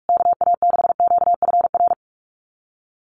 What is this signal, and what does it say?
Morse code, heard as a keyed tone, about 44 wpm, 700 Hz. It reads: KA6QFR
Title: USA Call Signs Repeated 3x 45wpm